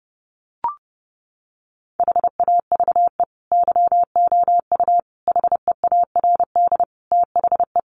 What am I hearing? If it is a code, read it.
Morse code, heard as a keyed tone, about 30 wpm, 700 Hz. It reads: HAVE YOU HEARD THE
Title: Sentences from Top 1400 Words 30wpm